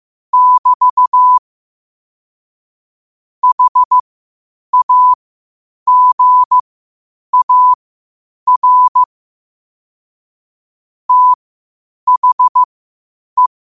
Morse code, heard as a keyed tone, about 15 wpm, spaced as for 8 wpm, 1000 Hz. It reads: <BT> HAGAR THE